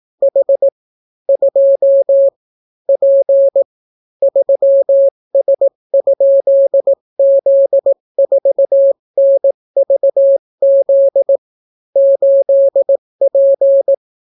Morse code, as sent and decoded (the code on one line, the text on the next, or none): .... / ..--- / .--. / ...-- ... ..--.. --.. ....- -. ...- --.. / ---.. .--.
H 2 P 3S?Z4NVZ 8P